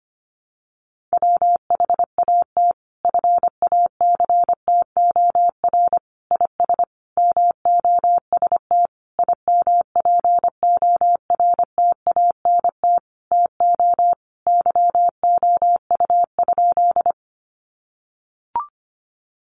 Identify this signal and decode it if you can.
Morse code, heard as a keyed tone, about 25 wpm, 700 Hz. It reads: WHAT FACTOR IS MOST IMPORTANT TO YOU?
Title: Sentences from Top 1300 Words 25wpm